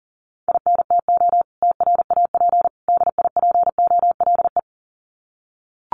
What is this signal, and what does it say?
Morse code, heard as a keyed tone, about 40 wpm, 700 Hz. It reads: INTO TRAP DIPOLE
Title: Common QSO Phrases 40wpm